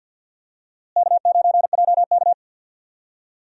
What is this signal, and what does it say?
Morse code, heard as a keyed tone, about 52 wpm, 700 Hz. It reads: K9JK
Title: USA Call Signs - 4 Characters - Repeated 2x - 50wpm